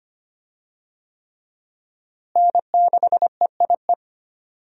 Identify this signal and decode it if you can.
Morse code, heard as a keyed tone, about 25 wpm, 700 Hz. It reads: N6EIE
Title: USA Call Signs - 5 Characters - Repeated 3x - 25wpm